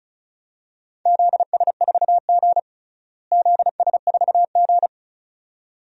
Morse code, heard as a keyed tone, about 36 wpm, 700 Hz. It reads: ZS4G ZS4G
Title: DXCC-100 Call Signs - 4 Characters - Repeated 2x - 35wpm